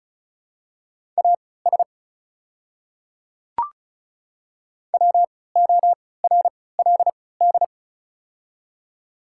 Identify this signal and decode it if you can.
Morse code, heard as a keyed tone, about 36 wpm, spaced as for 20 wpm, 700 Hz. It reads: AS WORLD
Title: 3x Character Spacing - Top 200 Words - 35wpm